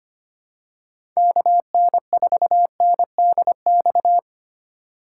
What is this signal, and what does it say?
Morse code, heard as a keyed tone, about 25 wpm, 700 Hz. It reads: KN4NDX
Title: USA Call Signs - 6 Characters - 2x Character Spacing - 25wpm